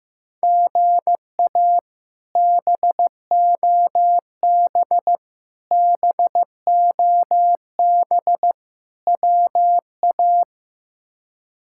Morse code, heard as a keyed tone, about 15 wpm, 700 Hz. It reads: GA BOB BOB WA
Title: K1USN SST QSO Practice - 15wpm